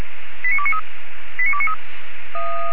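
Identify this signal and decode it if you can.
Five-tone selective call (ZVEI), heard as quick runs of five tones, tones of about 70 ms. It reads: 89293
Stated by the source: ZVEI five-tone selective call received on SDR, from feuerwehr-orlamuende.org (fire brigade Orlamünde).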